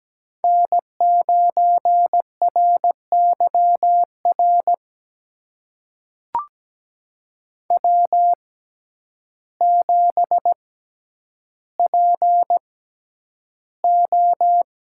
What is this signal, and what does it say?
Morse code, heard as a keyed tone, about 17 wpm, spaced as for 9 wpm, 700 Hz. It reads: N9RYR W7PO
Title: USA Call Signs - 5 Characters - 6x Character Spacing - 17wpm